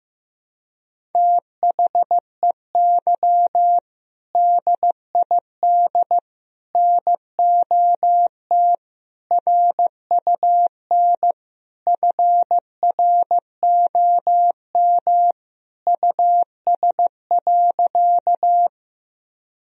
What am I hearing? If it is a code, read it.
Morse code, heard as a keyed tone, about 15 wpm, 700 Hz. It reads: THEY DID NOT RUN FROM US.
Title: Sentences from Top 1000 Words - Review Part 1 - 15wpm